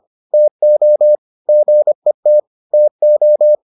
Morse code, heard as a keyed tone, about 25 wpm, 600 Hz. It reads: TO GET TO